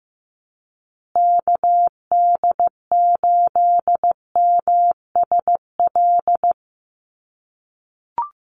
Morse code, heard as a keyed tone, about 15 wpm, 700 Hz. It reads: KD8MSL
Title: USA Call Signs Repeated 3x 15wpm